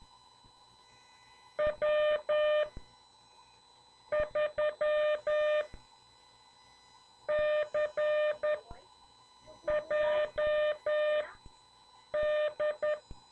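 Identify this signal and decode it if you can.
Morse code, heard as a keyed tone, about 10 wpm, spaced as for 5 wpm, 600 Hz. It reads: W3CJD
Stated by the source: CW class lesson from The Martian Net with instructor N2KPS.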